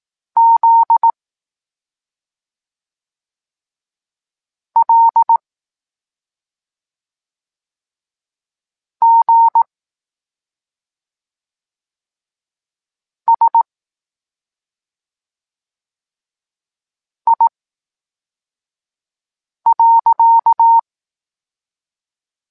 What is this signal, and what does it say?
Morse code, heard as a keyed tone, about 18 wpm, spaced as for 4 wpm, 920 Hz. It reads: Z L G S I.
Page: Tekst 23 – Doskonalenie odbioru poznanych liter